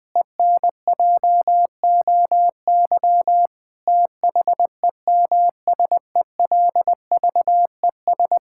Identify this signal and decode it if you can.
Morse code, heard as a keyed tone, about 20 wpm, 700 Hz. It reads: ENJOY THEMSELVES
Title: Sentences from Top 1000 Words - Review Part 5 - 20wpm